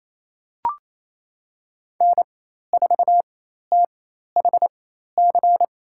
Morse code, heard as a keyed tone, about 28 wpm, spaced as for 13 wpm, 700 Hz. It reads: N4THC
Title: USA Call Signs - 5 Characters - 4x Character Spacing - 28wpm